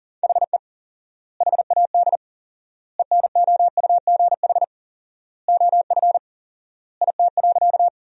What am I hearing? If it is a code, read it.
Morse code, heard as a keyed tone, about 40 wpm, 700 Hz. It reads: HE HAD ENOUGH OF IT.
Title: 4x Word-Spacing - Sentences from Top 400 Words - 40wpm